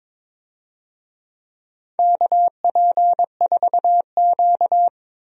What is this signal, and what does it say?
Morse code, heard as a keyed tone, about 22 wpm, 700 Hz. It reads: KP4Q
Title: DXCC-100 Call Signs - 4 Characters - 22wpm